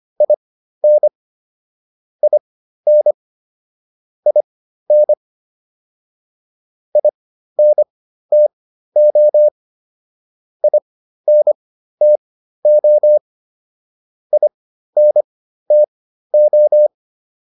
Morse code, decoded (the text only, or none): IN IN IN INTO INTO INTO